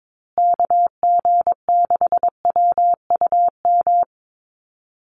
Morse code, heard as a keyed tone, about 22 wpm, 690 Hz. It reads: KG6WUM
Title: USA Call Signs Repeated 3x 22wpm